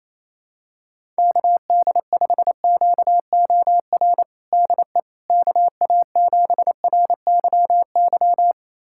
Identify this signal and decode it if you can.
Morse code, heard as a keyed tone, about 28 wpm, 700 Hz. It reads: KD5QOR DE KA7RYY
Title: USA Callsign DE Callsign - 6 Characters - 28wpm